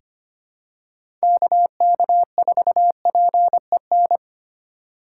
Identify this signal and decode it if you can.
Morse code, heard as a keyed tone, about 25 wpm, 700 Hz. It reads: KK4PEN
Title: USA Call Signs - 6 Characters - 3x Character Spacing - 25wpm